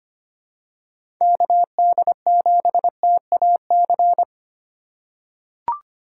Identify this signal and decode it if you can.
Morse code, heard as a keyed tone, about 25 wpm, 700 Hz. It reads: KD7TAC
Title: USA Call Signs - 6 Characters - 4x Character Spacing - 25wpm